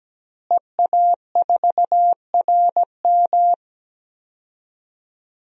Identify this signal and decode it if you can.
Morse code, heard as a keyed tone, about 17 wpm, 700 Hz. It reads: EA4RM
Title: DXCC-100 Call Signs - 5 Characters - Repeated 3x - 17wpm